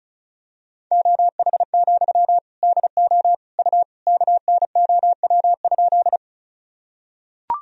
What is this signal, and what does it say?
Morse code, heard as a keyed tone, about 34 wpm, 700 Hz. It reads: OH, DO U KNOW?
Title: Sentences from Top 500 Words and Abbreviations 35wpm